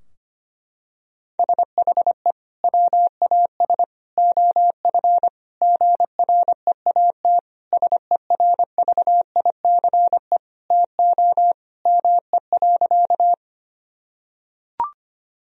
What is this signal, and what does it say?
Morse code, heard as a keyed tone, about 25 wpm, 700 Hz. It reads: SHE WAS OF GREAT SERVICE TO ME.
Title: Sentences from Top 1000 Words - Review Part 4 - 25wpm